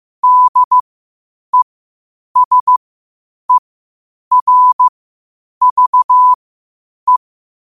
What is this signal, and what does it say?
Morse code, heard as a keyed tone, about 15 wpm, spaced as for 8 wpm, 1000 Hz. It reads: DESERVE
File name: qotdfiles / 8wpm